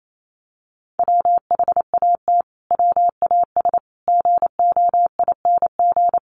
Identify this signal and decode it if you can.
Morse code, heard as a keyed tone, about 28 wpm, 700 Hz. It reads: WHAT WAS GOING
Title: Sets of 4 Words 28wpm